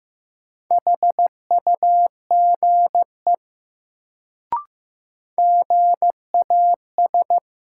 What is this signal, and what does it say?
Morse code, heard as a keyed tone, about 15 wpm, 700 Hz. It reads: HUGE GAS
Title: Sentences from Top 1100 Words - New Word Drill - 15wpm